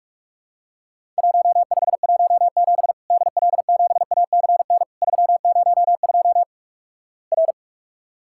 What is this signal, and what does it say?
Morse code, heard as a keyed tone, about 46 wpm, 700 Hz. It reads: 1517 DL7ACN 302 R
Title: Worked All Europe DX Contest - QTC Exchange - 45wpm